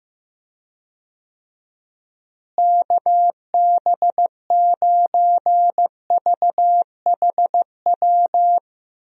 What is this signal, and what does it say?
Morse code, heard as a keyed tone, about 15 wpm, 700 Hz. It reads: KB9VHW
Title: USA Call Signs - 6 Characters - Repeated 2x - 15wpm